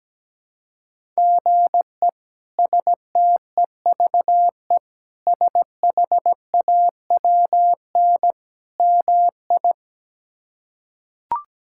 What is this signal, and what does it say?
Morse code, heard as a keyed tone, about 17 wpm, 700 Hz. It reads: GE STEVE SHAWN MI
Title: K1USN SST QSO Practice - 17wpm